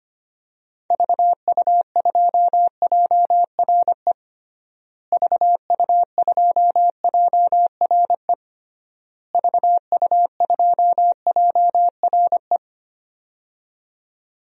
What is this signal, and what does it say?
Morse code, heard as a keyed tone, about 25 wpm, 700 Hz. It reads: VU2JRE VU2JRE VU2JRE
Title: DXCC-100 Call Signs - 6 Characters - Repeated 3x - 25wpm